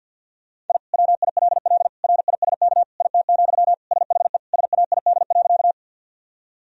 Morse code, heard as a keyed tone, about 46 wpm, 700 Hz. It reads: I WILL RISK IT, SHE SAID.
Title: Sentences from Top 1000 Words - Review Part 4 - 50wpm